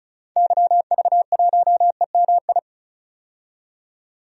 Morse code, heard as a keyed tone, about 36 wpm, 700 Hz. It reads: YV1EMI
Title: DXCC-100 Call Signs - 6 Characters - 35wpm